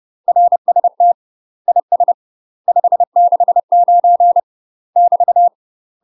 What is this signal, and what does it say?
Morse code, heard as a keyed tone, about 30 wpm, 700 Hz. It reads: RST IS 569 <BT>
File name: UK MASTER QSOs TRAINING/30WPM